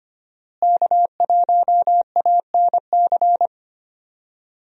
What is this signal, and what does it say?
Morse code, heard as a keyed tone, about 25 wpm, 700 Hz. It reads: K1ANC
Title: USA Call Signs - 5 Characters - Repeated 2x - 25wpm